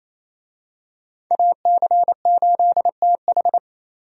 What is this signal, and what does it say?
Morse code, heard as a keyed tone, about 28 wpm, 700 Hz. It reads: AC8TH
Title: USA Call Signs - 5 Characters - 28wpm